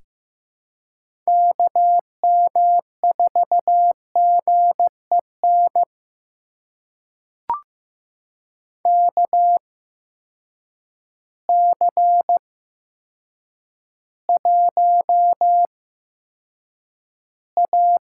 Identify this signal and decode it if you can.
Morse code, heard as a keyed tone, about 15 wpm, spaced as for 8 wpm, 700 Hz. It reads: KM4GEN KC1A
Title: USA Call Signs - 6 Characters - 8x Character Spacing - 15wpm